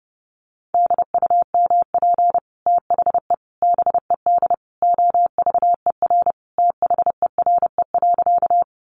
Morse code, heard as a keyed tone, about 30 wpm, 700 Hz. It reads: DUMP THE BED OVER THERE.
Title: Sentences from Top 1300 Words 30wpm